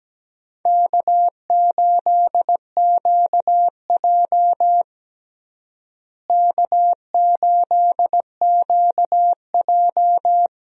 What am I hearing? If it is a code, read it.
Morse code, heard as a keyed tone, about 17 wpm, 700 Hz. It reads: K8QJ K8QJ
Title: USA Call Signs - 4 Characters - Repeated 3x - 17wpm